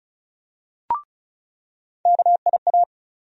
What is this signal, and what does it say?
Morse code, heard as a keyed tone, about 36 wpm, 700 Hz. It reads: KIA
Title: USA Call Sign Postfixes - 3 Letters 35wpm